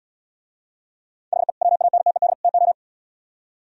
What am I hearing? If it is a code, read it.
Morse code, heard as a keyed tone, about 77 wpm, 700 Hz. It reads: HE WANTED TO